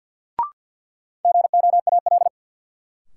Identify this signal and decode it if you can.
Morse code, heard as a keyed tone, about 53 wpm, 700 Hz. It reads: GOAL
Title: Sentences from Top 1400 Words - New Word Drill Alt - 50wpm